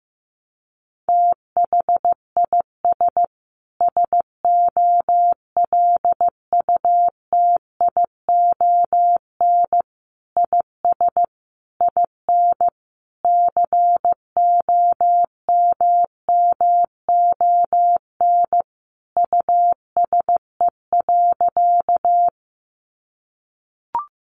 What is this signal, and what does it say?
Morse code, heard as a keyed tone, about 15 wpm, 700 Hz. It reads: THIS SOLUTION IS IN COMMON USE.
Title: Sentences from Top 1300 Words 15wpm